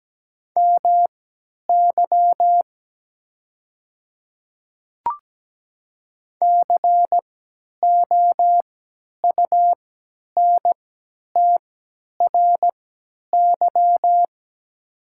Morse code, heard as a keyed tone, about 17 wpm, spaced as for 10 wpm, 700 Hz. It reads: MY COUNTRY
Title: 3x Character Spacing - Top 500 Words - 17wpm